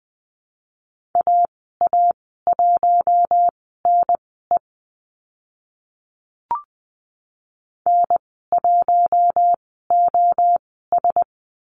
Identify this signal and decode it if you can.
Morse code, heard as a keyed tone, about 20 wpm, spaced as for 14 wpm, 700 Hz. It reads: AA1NE N1OS
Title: 042 Lesson 8g - Number 1 - Call Signs 20wpm